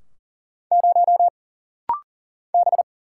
Morse code, heard as a keyed tone, about 39 wpm, 700 Hz. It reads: 0 B
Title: Single Letter-Number - Rapid-Fire 40wpm